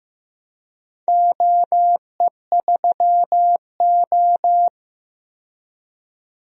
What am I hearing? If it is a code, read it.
Morse code, heard as a keyed tone, about 15 wpm, 700 Hz. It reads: OE3O
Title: DXCC-100 Call Signs - 4 Characters - 15wpm